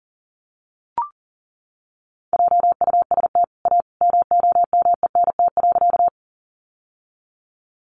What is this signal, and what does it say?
Morse code, heard as a keyed tone, about 40 wpm, 700 Hz. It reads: JUST A MOMENT.
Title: Sentences from Top 1000 Words 40wpm